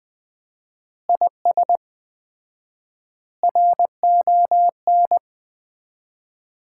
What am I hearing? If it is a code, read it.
Morse code, heard as a keyed tone, about 20 wpm, 700 Hz. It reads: IS RON
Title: Common QSO Phrases - 4x Word-Spacing - 20wpm